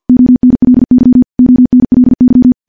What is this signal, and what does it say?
Morse code, heard as a keyed tone, about 50 wpm, 250 Hz. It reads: ONLY ONLY
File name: morse / morse-training-50WPM